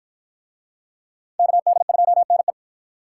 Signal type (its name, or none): Morse code